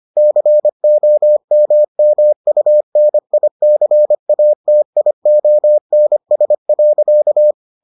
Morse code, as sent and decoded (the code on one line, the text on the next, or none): -.-. --- -- -- ..- -. .. -.-. .- - .. --- -. ... .-.-.-
COMMUNICATIONS.